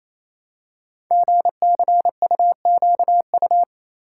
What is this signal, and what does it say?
Morse code, heard as a keyed tone, about 28 wpm, 700 Hz. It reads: GCUQU
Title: Code Groups - 5 Letters - 28wpm